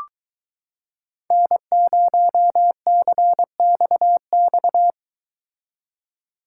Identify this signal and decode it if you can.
Morse code, heard as a keyed tone, about 23 wpm, 700 Hz. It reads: N0CXX